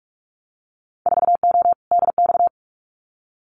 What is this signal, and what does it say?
Morse code, heard as a keyed tone, about 46 wpm, 690 Hz. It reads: 4O DX
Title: Winter Field Day Exchanges Encore 45wpm